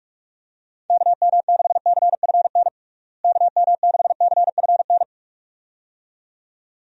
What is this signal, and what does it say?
Morse code, heard as a keyed tone, about 43 wpm, 700 Hz. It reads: KM6CFN KM6CFN
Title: USA Call Signs - 6 Characters - Repeated 2x - 45wpm